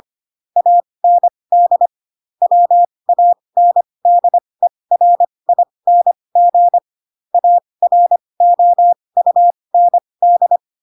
Morse code, heard as a keyed tone, about 25 wpm, spaced as for 20 wpm, 700 Hz. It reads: AND WANDERING AROUND
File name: Pinocchio / Pinocchio ch4 15 25 ik1hge